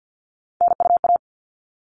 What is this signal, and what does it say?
Morse code, heard as a keyed tone, about 50 wpm, 700 Hz. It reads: NUA
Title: All Cut Numbers - 3-Digit 50wpm